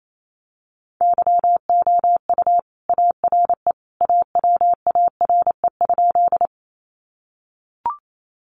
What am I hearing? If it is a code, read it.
Morse code, heard as a keyed tone, about 28 wpm, 700 Hz. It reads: YOU ARE AWARE?
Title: Sentences from Top 1200 Words 28wpm